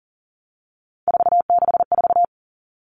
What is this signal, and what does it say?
Morse code, heard as a keyed tone, about 40 wpm, 700 Hz. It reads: 464
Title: Numbers - 3-Digit 40wpm